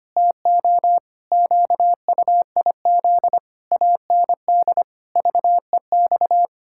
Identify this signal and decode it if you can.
Morse code, heard as a keyed tone, about 25 wpm, 700 Hz. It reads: TO QUIZ AND VEX
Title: Pangram Challenge v2 - 25wpm